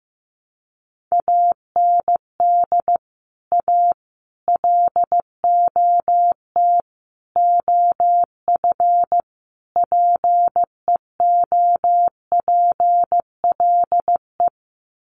Morse code, heard as a keyed tone, about 15 wpm, 700 Hz. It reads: AND A LOT OF PEOPLE
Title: Sets of 5 Words 15wpm